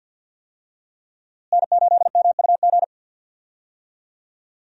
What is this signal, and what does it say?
Morse code, heard as a keyed tone, about 50 wpm, 700 Hz. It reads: N8MUG
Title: USA Call Signs - 5 Characters - Repeated 3x - 50wpm